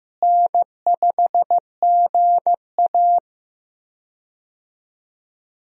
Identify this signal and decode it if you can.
Morse code, heard as a keyed tone, about 15 wpm, 700 Hz. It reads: N5GA
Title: USA Call Signs - 4 Characters - Repeated 3x - 15wpm